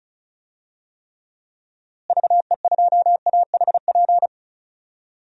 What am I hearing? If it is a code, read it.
Morse code, heard as a keyed tone, about 36 wpm, 700 Hz. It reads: VE2AHP
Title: DXCC-100 Call Signs - 6 Characters - Repeated 2x - 35wpm